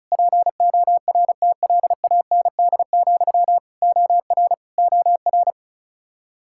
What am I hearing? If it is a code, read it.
Morse code, heard as a keyed tone, about 36 wpm, 700 Hz. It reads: PORTLAND, OR OR
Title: Practice QSO - Exchange 1 - Beginning - 35wpm